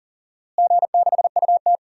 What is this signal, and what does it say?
Morse code, heard as a keyed tone, about 40 wpm, 700 Hz. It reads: G6UT
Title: DXCC-100 Call Signs - 4 Characters - Repeated 3x - 40wpm